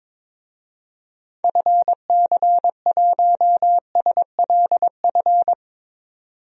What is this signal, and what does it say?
Morse code, heard as a keyed tone, about 21 wpm, 700 Hz. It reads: FC1SLF
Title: DXCC-100 Call Signs - 6 Characters - 22wpm